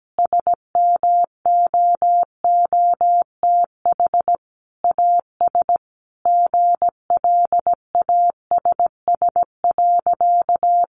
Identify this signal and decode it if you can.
Morse code, heard as a keyed tone, about 17 wpm, 700 Hz. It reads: SMOOTH AS GLASS.
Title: Sentences from Top 1400 Words 17wpm